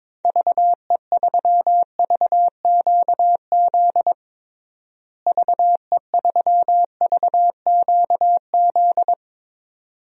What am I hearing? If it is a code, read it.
Morse code, heard as a keyed tone, about 22 wpm, 700 Hz. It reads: VE3VQZ VE3VQZ
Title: DXCC-100 Call Signs - 6 Characters - Repeated 2x - 22wpm